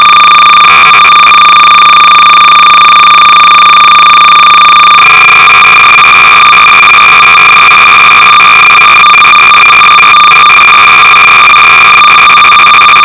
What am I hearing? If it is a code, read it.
Cassette data (Kansas City type).
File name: sorcerer / 300baud / Plot-(1979)(Dick Tolomei)(Quality Software)-BASIC-Sorcerer-300baud